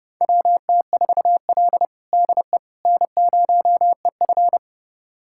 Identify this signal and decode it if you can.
Morse code, heard as a keyed tone, about 31 wpm, 700 Hz. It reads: WT4L DE N0EF
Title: USA Callsign DE Callsign - 4 Characters - 30wpm